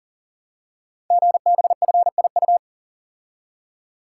Morse code, heard as a keyed tone, about 39 wpm, 690 Hz. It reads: GBFIU
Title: Code Groups - 5 Letters - 40wpm